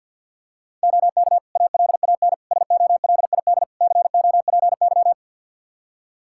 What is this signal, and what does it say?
Morse code, heard as a keyed tone, about 48 wpm, 700 Hz. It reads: OK ALAN SOLID COPY
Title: Practice QSO - Exchange 3 - More Info - 50wpm